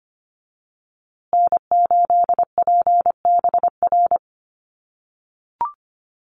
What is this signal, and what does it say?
Morse code, heard as a keyed tone, about 25 wpm, 700 Hz. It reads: N8PBR
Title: USA Call Signs Repeated 2x 25wpm